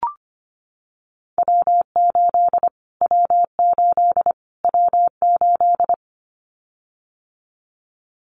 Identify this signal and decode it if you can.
Morse code, heard as a keyed tone, about 25 wpm, 700 Hz. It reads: W8 W8 W8
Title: Callsign Prefixes for Top 25 DXCC - Repeated 3x - 25wpm